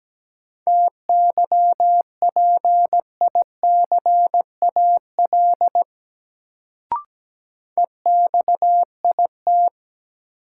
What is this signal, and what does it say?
Morse code, heard as a keyed tone, about 17 wpm, 700 Hz. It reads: TYPICAL EXIT
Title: Sentences from Top 1100 Words - New Word Drill - 17wpm